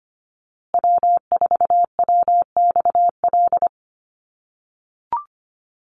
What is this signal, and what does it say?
Morse code, heard as a keyed tone, about 25 wpm, 700 Hz. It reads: W4WXL
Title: USA Call Signs Repeated 2x 25wpm